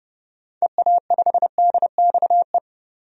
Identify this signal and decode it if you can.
Morse code, heard as a keyed tone, about 30 wpm, 700 Hz. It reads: EA5DXE
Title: DXCC-100 Call Signs - 6 Characters - 30wpm